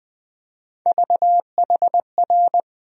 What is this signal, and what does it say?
Morse code, heard as a keyed tone, about 21 wpm, 700 Hz. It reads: VHR
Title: USA Call Sign Postfixes - 3 Letters 20wpm